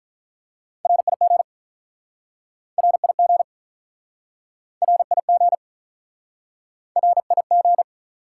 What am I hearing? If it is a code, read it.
Morse code, heard as a keyed tone, about 50 wpm, 700 Hz. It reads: RIG RIG RIG RIG